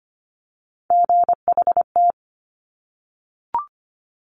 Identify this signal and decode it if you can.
Morse code, heard as a keyed tone, about 25 wpm, 700 Hz. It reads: GHT
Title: Top Three Letter Combinations in Words 25wpm